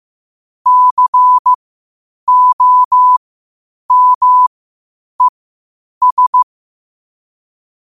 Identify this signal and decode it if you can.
Morse code, heard as a keyed tone, about 15 wpm, spaced as for 8 wpm, 1000 Hz. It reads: COMES
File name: qotdfiles / 8wpm